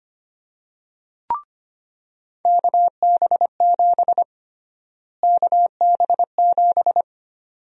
Morse code, heard as a keyed tone, about 25 wpm, 700 Hz. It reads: KB7 KB7
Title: USA Call Sign Prefixes - Repeated 2x - 25wpm